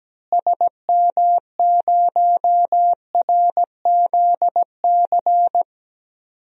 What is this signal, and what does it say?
Morse code, heard as a keyed tone, about 17 wpm, 700 Hz. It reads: SM0RZC
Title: DXCC-100 Call Signs - 6 Characters - Repeated 2x - 17wpm